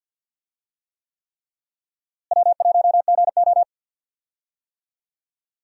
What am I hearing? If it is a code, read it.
Morse code, heard as a keyed tone, about 53 wpm, 700 Hz. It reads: W1GO
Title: USA Call Signs - 4 Characters - Repeated 3x - 50wpm